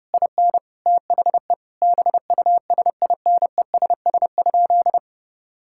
Morse code, heard as a keyed tone, about 30 wpm, 700 Hz. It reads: IN THE BUSINESS?
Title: Sentences from Top 1000 Words - Review Part 6 - 30wpm